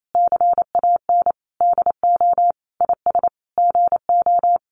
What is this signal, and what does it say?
Morse code, heard as a keyed tone, about 28 wpm, 700 Hz. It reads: CAN DO IS GO
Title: Sentences from Top 200 Words 28wpm